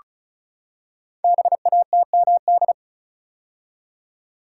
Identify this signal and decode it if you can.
Morse code, heard as a keyed tone, about 36 wpm, 700 Hz. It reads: BATMD
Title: Code Groups - 5 Letters - 35wpm